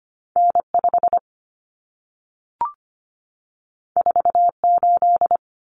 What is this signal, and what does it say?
Morse code, heard as a keyed tone, about 25 wpm, 700 Hz. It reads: N5 48
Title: Common Cut Numbers - 2-Digit 25wpm